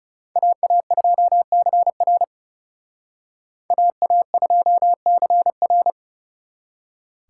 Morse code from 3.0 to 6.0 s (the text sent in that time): AA2CR